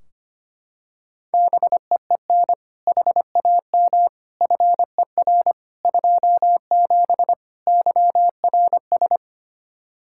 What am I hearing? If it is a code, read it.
Morse code, heard as a keyed tone, about 25 wpm, 700 Hz. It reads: BEEN HAM FER 27 YRS
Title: Practice QSO - Exchange 3 - More Info - 25wpm